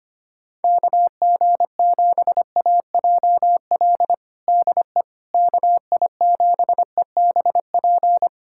Morse code, heard as a keyed tone, about 25 wpm, 700 Hz. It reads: KG7AJL DE KI7EBP
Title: USA Callsign DE Callsign - 6 Characters - 25wpm